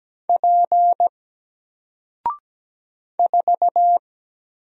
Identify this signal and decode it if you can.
Morse code, heard as a keyed tone, about 17 wpm, 700 Hz. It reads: P 4
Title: Single Letter-Number 17wpm